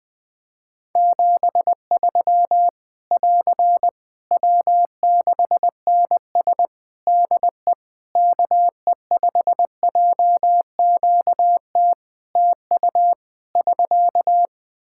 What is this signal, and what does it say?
Morse code, heard as a keyed tone, about 20 wpm, 700 Hz. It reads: 73 <AR> W6NS DE KE5JQT TU <SK>
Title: Practice QSO - Exchange 4 - Ending - 20wpm